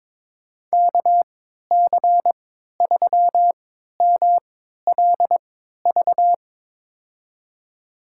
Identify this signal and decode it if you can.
Morse code, heard as a keyed tone, about 21 wpm, spaced as for 12 wpm, 700 Hz. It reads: KC3MLV
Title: USA Call Signs - 6 Characters - 3x Character Spacing - 22wpm